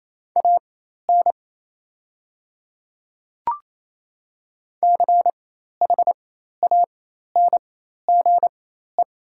Morse code, heard as a keyed tone, about 28 wpm, spaced as for 13 wpm, 690 Hz. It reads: AN CHANGE
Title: 4x Character Spacing - Top 200 Words - 28wpm